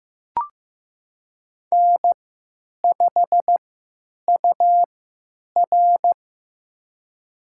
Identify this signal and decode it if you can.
Morse code, heard as a keyed tone, about 15 wpm, spaced as for 9 wpm, 700 Hz. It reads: N5UR
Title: USA Call Signs - 4 Characters - 3x Character Spacing - 15wpm